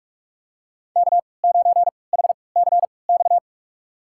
Morse code, heard as a keyed tone, about 42 wpm, spaced as for 25 wpm, 700 Hz. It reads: K9HCX
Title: USA Call Signs - 3x Character Spacing - 45wpm